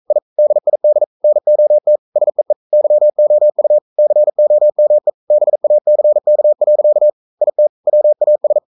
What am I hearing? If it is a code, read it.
Morse code, heard as a keyed tone, about 42 wpm, 600 Hz. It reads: I DID NOT SEE YOU COME BACK. IT WAS
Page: Romanen Never af Ken Follet på engelsk.